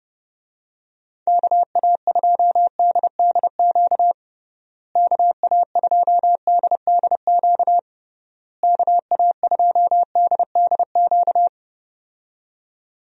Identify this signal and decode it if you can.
Morse code, heard as a keyed tone, about 30 wpm, 700 Hz. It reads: KA2DDQ KA2DDQ KA2DDQ
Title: USA Call Signs - 6 Characters - Repeated 3x - 30wpm